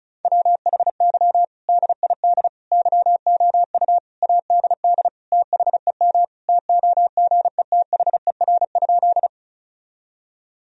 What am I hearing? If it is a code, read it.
Morse code, heard as a keyed tone, about 34 wpm, 700 Hz. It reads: WHY DID YOU ADD THEM TOGETHER?
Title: Sentences from Top 1000 Words - Review Part 1 - 35wpm